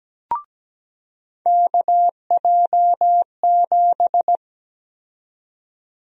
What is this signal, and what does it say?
Morse code, heard as a keyed tone, about 17 wpm, 700 Hz. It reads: KJ7
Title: USA Call Sign Prefixes - 17wpm